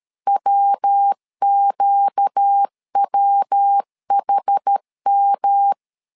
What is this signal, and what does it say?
Morse code, heard as a keyed tone, about 13 wpm, 800 Hz. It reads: WQWHM
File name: LEZIONI / 4 Q 12wpm 1833